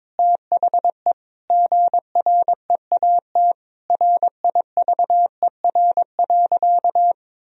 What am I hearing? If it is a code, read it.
Morse code, heard as a keyed tone, about 22 wpm, 700 Hz. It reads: THE GREAT RIVER.
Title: Sentences from Top 1000 Words - Review Part 1 - 22wpm